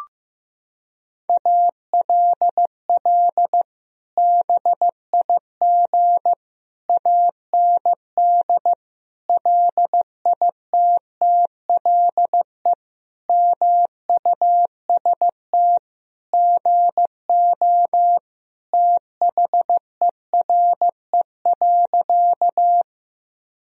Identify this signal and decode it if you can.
Morse code, heard as a keyed tone, about 15 wpm, 700 Hz. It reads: ALL BIG AND LITTLE MUST GO THERE.
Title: Sentences from Top 1000 Words - Review Part 1 - 15wpm